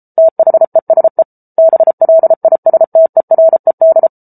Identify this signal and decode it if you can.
Morse code, heard as a keyed tone, about 34 wpm, 650 Hz. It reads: THESE BLISTERED